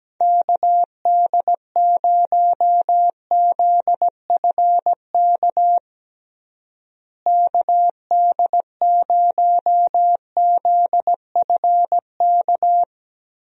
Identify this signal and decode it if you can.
Morse code, heard as a keyed tone, about 17 wpm, 700 Hz. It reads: KD0ZFK KD0ZFK
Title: USA Call Signs - 6 Characters - Repeated 3x - 17wpm